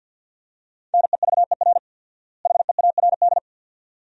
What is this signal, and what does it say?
Morse code, heard as a keyed tone, about 50 wpm, 700 Hz. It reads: NEVER HEARD
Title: 4x Word-Spacing - Sentences from Top 400 Words - 50wpm